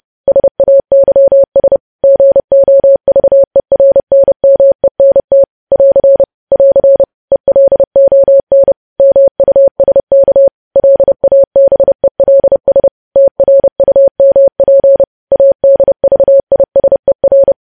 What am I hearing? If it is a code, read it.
Morse code, heard as a keyed tone, about 30 wpm, 550 Hz. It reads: SAYS GOVERNMENT <AR> <AR> ELON MUSK LABELS TRUMP ADVISER
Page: news headlines in morse code at 30 wpm